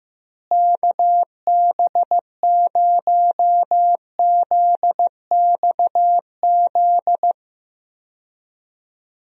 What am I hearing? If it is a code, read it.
Morse code, heard as a keyed tone, about 15 wpm, 700 Hz. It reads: KB0ZXZ
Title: USA Call Signs - 6 Characters - Repeated 3x - 15wpm